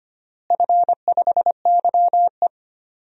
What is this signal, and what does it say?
Morse code, heard as a keyed tone, about 25 wpm, 700 Hz. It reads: F5YE
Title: DXCC-100 Call Signs - 4 Characters - Repeated 3x - 25wpm